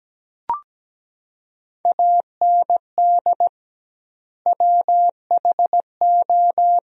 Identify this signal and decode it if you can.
Morse code, heard as a keyed tone, about 17 wpm, 700 Hz. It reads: AND WHO
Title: 2x Word-Spacing - Sentences from Top 100 Words - 17wpm